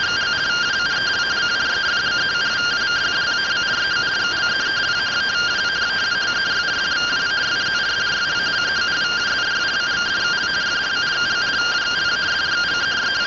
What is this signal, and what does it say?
Two-tone teleprinter signal, about 170 Hz shift.